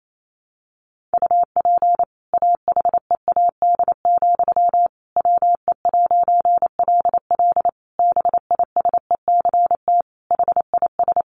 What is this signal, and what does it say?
Morse code, heard as a keyed tone, about 28 wpm, 700 Hz. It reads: UP AHEAD, WE'LL BISECT HIS
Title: A Sound Of Thunder 28wpm